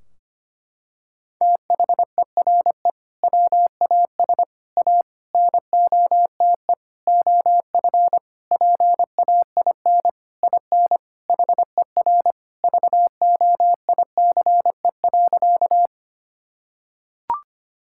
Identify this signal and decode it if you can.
Morse code, heard as a keyed tone, about 25 wpm, 700 Hz. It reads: THERE WAS A NOTE OF PAIN IN HER VOICE.
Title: Sentences from Top 1600 Words 25wpm